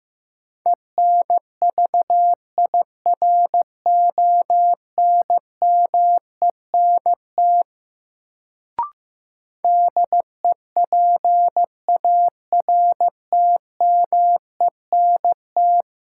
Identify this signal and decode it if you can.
Morse code, heard as a keyed tone, about 15 wpm, 700 Hz. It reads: ENVIRONMENT DEPARTMENT
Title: Sentences from Top 1000 Words - Review Part 5 - New Word Drill - 15wpm